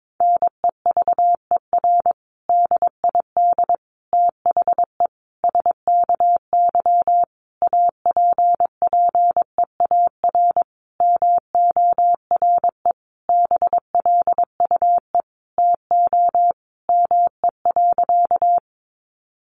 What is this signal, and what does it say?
Morse code, heard as a keyed tone, about 22 wpm, 700 Hz. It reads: NEVER DID THE SKY APPEAR MORE BLUE TO ME.
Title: Sentences from Top 1200 Words 22wpm